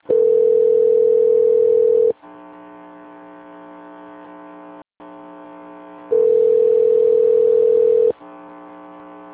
Ringback tone (North America), 440 + 480 Hz.